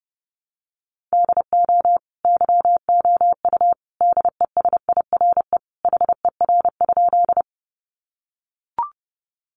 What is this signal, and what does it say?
Morse code, heard as a keyed tone, about 30 wpm, 700 Hz. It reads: DO YOU DESIRE HER?
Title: Sentences from Top 1100 Words 30wpm